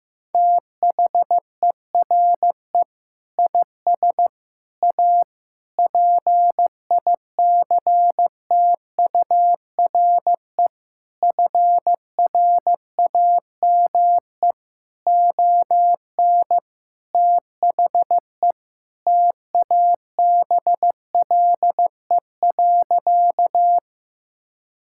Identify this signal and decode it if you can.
Morse code, heard as a keyed tone, about 15 wpm, 700 Hz. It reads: THERE IS A PICTURE FRAME ON THE TABLE.
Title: Sentences from Top 1000 Words - Review Part 6 - 15wpm